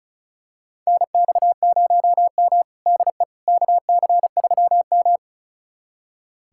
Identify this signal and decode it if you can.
Morse code, heard as a keyed tone, about 36 wpm, 690 Hz. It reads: NX0M DE KC3M
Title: USA Callsign DE Callsign - 4 Characters - 35wpm